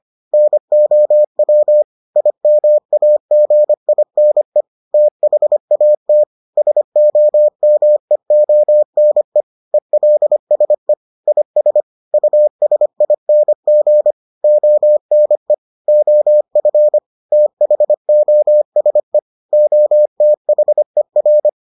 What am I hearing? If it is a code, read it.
Morse code, heard as a keyed tone, about 25 wpm, 600 Hz. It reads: NOW IMAGINE THAT SOMEONE ELSE IS USING ONE OF THOSE OTHER